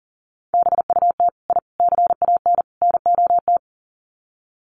Morse code, heard as a keyed tone, about 40 wpm, 700 Hz. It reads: BUT I CAN NOT
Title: Sets of 4 Words 40wpm